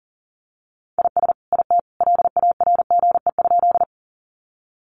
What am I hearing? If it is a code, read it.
Morse code, heard as a keyed tone, about 40 wpm, 700 Hz. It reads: IS IT LARGE?
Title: Sentences from Top 200 Words 40wpm